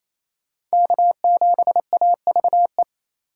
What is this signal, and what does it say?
Morse code, heard as a keyed tone, about 34 wpm, 700 Hz. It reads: K7AVE
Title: Speed Racing - USA 1x3 Call Signs 28wpm